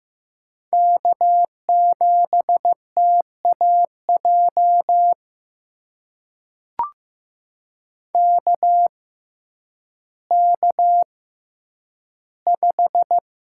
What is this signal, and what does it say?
Morse code, heard as a keyed tone, about 15 wpm, spaced as for 9 wpm, 700 Hz. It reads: K7TAJ KK5
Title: USA Call Signs - 5 Characters - 6x Character Spacing - 15wpm